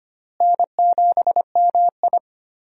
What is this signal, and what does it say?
Morse code, heard as a keyed tone, about 25 wpm, 700 Hz. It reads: N7MI
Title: USA Call Signs - 4 Characters - Repeated 2x - 25wpm